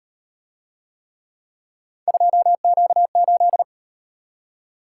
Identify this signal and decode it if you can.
Morse code, heard as a keyed tone, about 38 wpm, 700 Hz. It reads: 2Q8